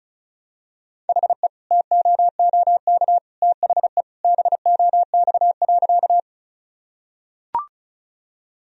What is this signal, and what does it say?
Morse code, heard as a keyed tone, about 36 wpm, 700 Hz. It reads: HE TOOK THE BOX.
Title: Sentences from Top 1000 Words - Review Part 2 - 35wpm